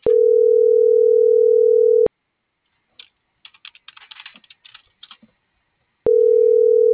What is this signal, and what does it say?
Ringback tone (North America), 440 + 480 Hz.